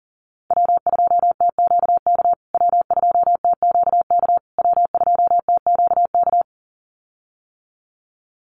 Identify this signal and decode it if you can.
Morse code, heard as a keyed tone, about 40 wpm, 700 Hz. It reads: W2TQK W2TQK W2TQK
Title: USA Call Signs Repeated 3x 40wpm